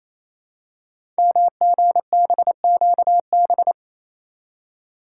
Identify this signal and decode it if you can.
Morse code, heard as a keyed tone, about 27 wpm, 700 Hz. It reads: MGBQB
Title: Code Groups - 5 Letters - 28wpm